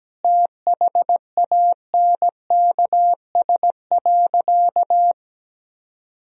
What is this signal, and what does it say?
Morse code, heard as a keyed tone, about 17 wpm, 700 Hz. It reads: THANKS.
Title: Sentences from Top 1000 Words - Review Part 3 - 17wpm